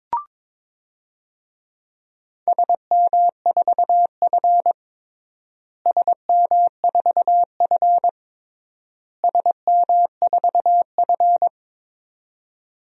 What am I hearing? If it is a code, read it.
Morse code, heard as a keyed tone, about 22 wpm, 700 Hz. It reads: SM4F SM4F SM4F
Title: DXCC-100 Call Signs - 4 Characters - Repeated 3x - 22wpm